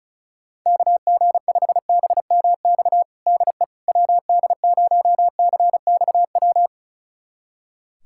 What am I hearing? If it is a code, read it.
Morse code, heard as a keyed tone, about 36 wpm, 700 Hz. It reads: KG5BMX DE WD0CXW
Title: USA Callsign DE Callsign - 6 Characters - 35wpm